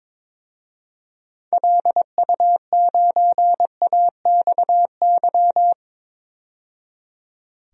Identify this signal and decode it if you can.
Morse code, heard as a keyed tone, about 22 wpm, 700 Hz. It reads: LU9AXY
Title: DXCC-100 Call Signs - 6 Characters - 22wpm